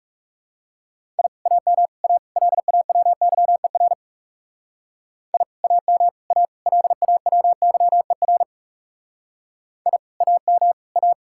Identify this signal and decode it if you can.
Morse code, heard as a keyed tone, about 36 wpm, 700 Hz. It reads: I AM A LAWYER I AM A LAWYER I AM A
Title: Speed-Racing Common QSO Phrases - 30wpm